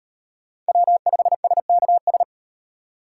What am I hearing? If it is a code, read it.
Morse code, heard as a keyed tone, about 38 wpm, 700 Hz. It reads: W5SKS